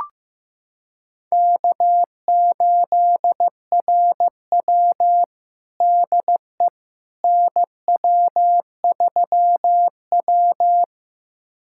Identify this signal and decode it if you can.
Morse code, heard as a keyed tone, about 15 wpm, 700 Hz. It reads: K8RW DE NW3W
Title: USA Callsign DE Callsign - 4 Characters - 15wpm